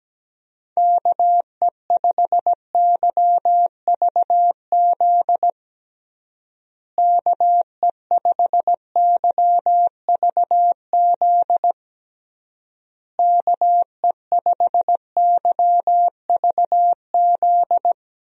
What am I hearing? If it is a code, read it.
Morse code, heard as a keyed tone, about 17 wpm, 700 Hz. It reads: KE5YVZ KE5YVZ KE5YVZ
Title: USA Call Signs - 6 Characters - Repeated 3x - 17wpm